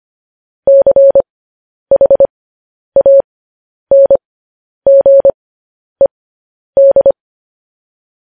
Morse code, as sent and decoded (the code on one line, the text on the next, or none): -.-. .... .- -. --. -..
CHANGED